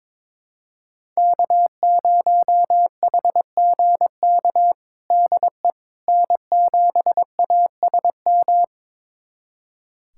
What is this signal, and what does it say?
Morse code, heard as a keyed tone, about 22 wpm, 700 Hz. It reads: K0HGK DE N7ASM
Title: USA Callsign DE Callsign - 5 Characters - 22wpm